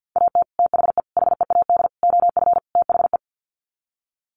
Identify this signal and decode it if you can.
Morse code, heard as a keyed tone, about 48 wpm, 700 Hz. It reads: AT THE HEAD OF THE
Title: Sets of 5 Words 50wpm